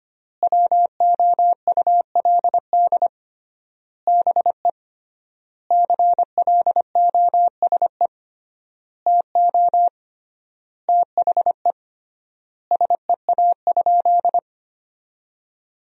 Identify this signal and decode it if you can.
Morse code, heard as a keyed tone, about 25 wpm, 700 Hz. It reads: WOULD BE CLOSE TO THE SEA?
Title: 3x Word-Spacing - Sentences from Top 300 Words - 25wpm